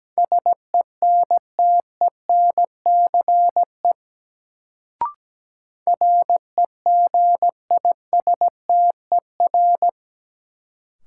Morse code, heard as a keyed tone, about 17 wpm, 700 Hz. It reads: SENTENCE REGISTER
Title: Sentences from Top 1200 Words - New Word Drill Alt - 17wpm